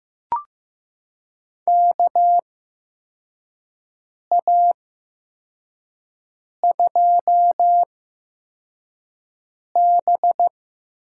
Morse code, heard as a keyed tone, about 16 wpm, spaced as for 4 wpm, 700 Hz. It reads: KA2B
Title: USA Call Signs - 6 Characters - 8x Character Spacing - 15wpm